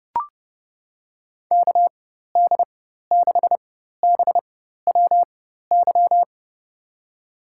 Morse code, heard as a keyed tone, about 30 wpm, spaced as for 14 wpm, 700 Hz. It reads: KD6BWY
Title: USA Call Signs - 6 Characters - 4x Character Spacing - 30wpm